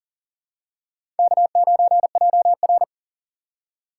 Morse code, heard as a keyed tone, about 37 wpm, 700 Hz. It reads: K9JR
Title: USA Call Signs - 2x Character Spacing - 40wpm